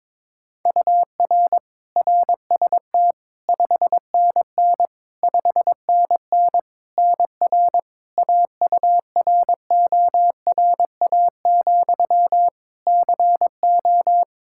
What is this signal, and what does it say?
Morse code, heard as a keyed tone, about 22 wpm, 700 Hz. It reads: UR RST 5NN 5NN NR AURORA, CO
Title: Practice QSO - Exchange 1 - Beginning - 22wpm